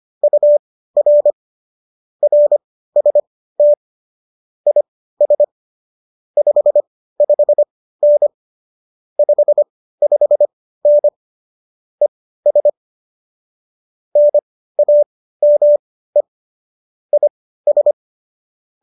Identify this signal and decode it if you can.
Morse code, heard as a keyed tone, about 25 wpm, spaced as for 15 wpm, 590 Hz. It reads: UR RST IS 55N 55N ES NAME IS
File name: qso207 15